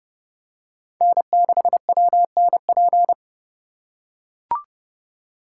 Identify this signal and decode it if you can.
Morse code, heard as a keyed tone, about 30 wpm, 700 Hz. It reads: N6WNP